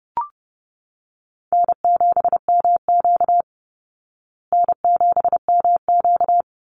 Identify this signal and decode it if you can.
Morse code, heard as a keyed tone, about 30 wpm, 700 Hz. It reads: N7MQ N7MQ
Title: Easy Call Signs Repeated 2x - 30wpm